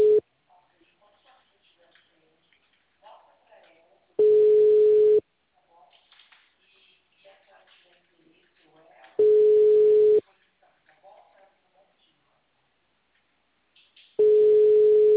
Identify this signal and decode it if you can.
Ringback tone (Europe, 425 Hz), with speech.